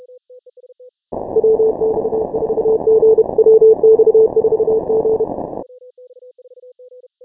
Morse code, heard as a keyed tone, about 32 wpm, 440 Hz. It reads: WX4G WX4G
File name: final / W1UJ-168-140-MorseRunner